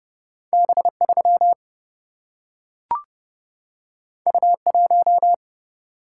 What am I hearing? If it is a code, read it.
Morse code, heard as a keyed tone, about 30 wpm, 700 Hz. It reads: B 3 U 1
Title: Single-Letter Single-Number 30wpm